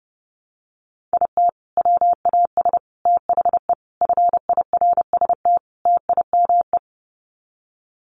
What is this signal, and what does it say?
Morse code, heard as a keyed tone, about 30 wpm, 700 Hz. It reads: IT WAS THE FIRST TIME
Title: Sets of 5 Words 30wpm